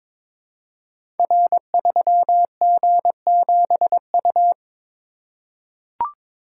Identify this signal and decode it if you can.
Morse code, heard as a keyed tone, about 22 wpm, 700 Hz. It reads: R3G7U
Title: Code Groups - 5 Letter:Numbers - 22wpm